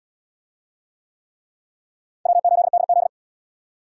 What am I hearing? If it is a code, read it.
Morse code, heard as a keyed tone, about 73 wpm, 700 Hz. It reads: W8DZ